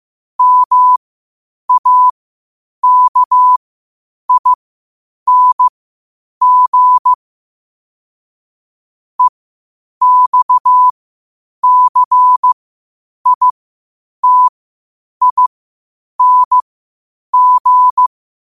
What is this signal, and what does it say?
Morse code, heard as a keyed tone, about 15 wpm, spaced as for 8 wpm, 1000 Hz. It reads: MAKING EXCITING
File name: qotdfiles / 8wpm